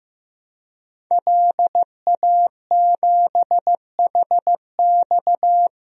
Morse code, heard as a keyed tone, about 15 wpm, 700 Hz. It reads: LA7HX
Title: DXCC-100 Call Signs - 5 Characters - 15wpm